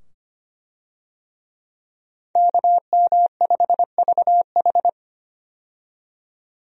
Morse code, heard as a keyed tone, about 25 wpm, 700 Hz. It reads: KM5VH
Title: USA Call Signs - 5 Characters - 25wpm